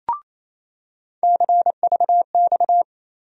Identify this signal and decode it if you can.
Morse code, heard as a keyed tone, about 28 wpm, 700 Hz. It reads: CVX
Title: USA Call Sign Postfixes - 3 Letters 28wpm